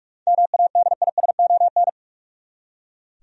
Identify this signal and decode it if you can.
Morse code, heard as a keyed tone, about 43 wpm, 700 Hz. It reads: MADISON
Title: USA Common Names - 45wpm